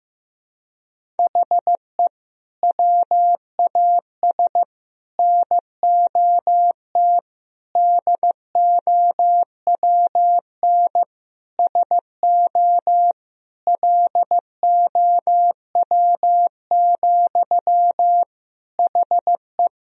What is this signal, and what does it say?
Morse code, heard as a keyed tone, about 15 wpm, 700 Hz. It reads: HE WAS NOT DOWN SO LOW, HE
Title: Sentences from Top 1000 Words - Review Part 1 - 15wpm